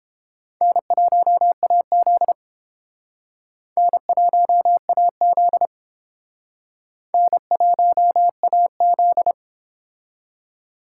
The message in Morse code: N1AZ N1AZ N1AZ

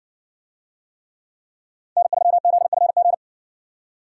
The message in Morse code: N3ZFG

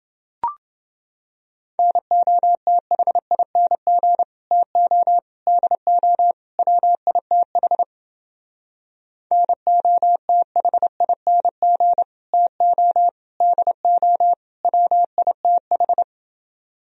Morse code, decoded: NOTHING TO DO WITH NOTHING TO DO WITH

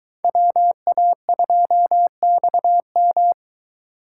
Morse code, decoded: WA2XM

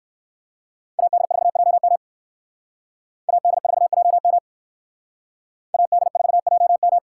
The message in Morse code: AD4JM AD4JM AD4JM

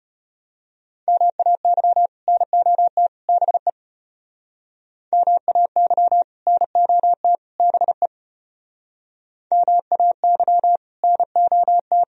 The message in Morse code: MAY NOT BE MAY NOT BE MAY NOT

